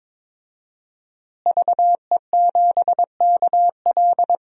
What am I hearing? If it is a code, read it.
Morse code, heard as a keyed tone, about 22 wpm, 700 Hz. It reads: VE7KL
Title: DXCC-100 Call Signs - 5 Characters - Repeated 3x - 22wpm